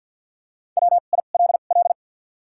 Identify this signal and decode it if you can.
Morse code, heard as a keyed tone, about 54 wpm, spaced as for 37 wpm, 700 Hz. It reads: WILL
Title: 2x Character Spacing - Top 500 Words - 50wpm